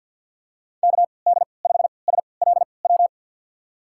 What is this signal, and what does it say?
Morse code, heard as a keyed tone, about 52 wpm, spaced as for 29 wpm, 700 Hz. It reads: KD5SLW
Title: USA Call Signs - 6 Characters - 3x Character Spacing - 50wpm